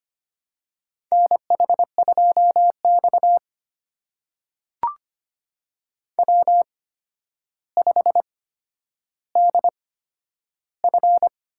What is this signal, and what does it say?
Morse code, heard as a keyed tone, about 25 wpm, spaced as for 11 wpm, 700 Hz. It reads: NH2X W5DF
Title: USA Call Signs - 4 Characters - 8x Character Spacing - 25wpm